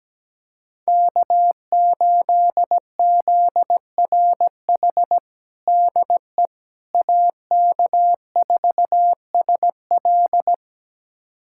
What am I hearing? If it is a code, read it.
Morse code, heard as a keyed tone, about 17 wpm, 700 Hz. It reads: K8ZRH DE AK4SL
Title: USA Callsign DE Callsign - 5 Characters - 17wpm